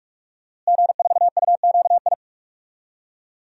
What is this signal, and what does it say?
Morse code, heard as a keyed tone, about 41 wpm, 700 Hz. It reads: G4UQI
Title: DXCC-100 Call Signs - 5 Characters - Repeated 2x - 45wpm